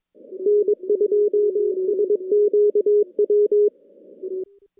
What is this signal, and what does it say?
Morse code, heard as a keyed tone, about 22 wpm, 410 Hz. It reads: N2SQW
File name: contests / CW-SS-k5tr- -2100